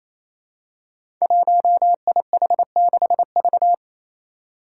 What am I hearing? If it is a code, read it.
Morse code, heard as a keyed tone, about 28 wpm, 700 Hz. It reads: 1IH6V
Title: Five Challenging Letter-Number Combinations 28wpm